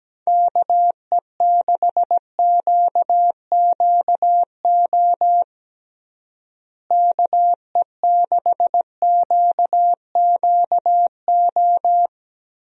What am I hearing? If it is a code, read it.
Morse code, heard as a keyed tone, about 17 wpm, 700 Hz. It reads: KE6QQO KE6QQO
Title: USA Call Signs - 6 Characters - Repeated 3x - 17wpm